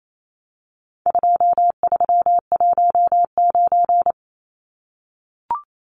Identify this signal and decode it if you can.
Morse code, heard as a keyed tone, about 28 wpm, 700 Hz. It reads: 2319
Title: Numbers - 4-Digit 28wpm